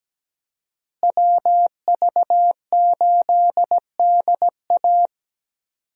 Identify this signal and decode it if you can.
Morse code, heard as a keyed tone, about 17 wpm, 700 Hz. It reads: WV8DA
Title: USA Call Signs - 5 Characters - Repeated 3x - 17wpm